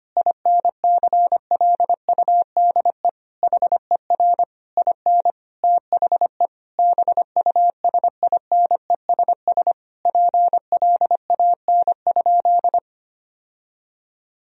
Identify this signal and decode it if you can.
Morse code, heard as a keyed tone, about 25 wpm, 700 Hz. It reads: INCLUDE HER IN THE BUSINESS PLAN?
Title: Sentences from Top 1000 Words - Review Part 3 - 25wpm